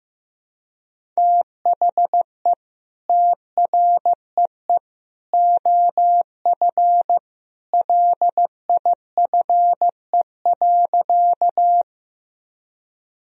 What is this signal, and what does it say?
Morse code, heard as a keyed tone, about 15 wpm, 700 Hz. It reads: THE TREE OF LIFE.
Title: Sentences from Top 1000 Words - Review Part 1 - 15wpm